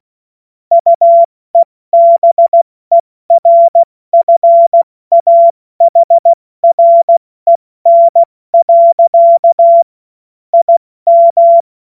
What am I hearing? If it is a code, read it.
Morse code, heard as a keyed tone, about 16 wpm, 680 Hz. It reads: UEBERFAHREN. IM
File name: Morsekurs / Witze / Witz 20. September 2022 70 F80